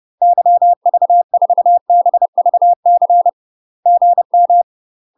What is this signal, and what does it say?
Morse code, heard as a keyed tone, about 30 wpm, 700 Hz. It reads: YV4BVC GM
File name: UK MASTER QSOs TRAINING / 30WPM / UK MASTER QSOs 0042